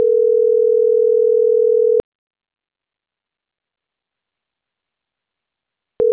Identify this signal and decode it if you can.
Ringback tone (North America), 440 + 480 Hz.